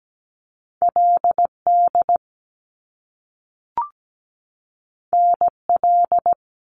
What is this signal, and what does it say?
Morse code, heard as a keyed tone, about 17 wpm, 700 Hz. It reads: LD NL
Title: Top Two Letter Combinations in Words 17wpm